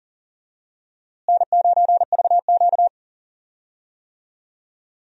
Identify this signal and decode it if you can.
Morse code, heard as a keyed tone, about 40 wpm, 700 Hz. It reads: N9VQ
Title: USA Call Signs - 4 Characters - Repeated 3x - 40wpm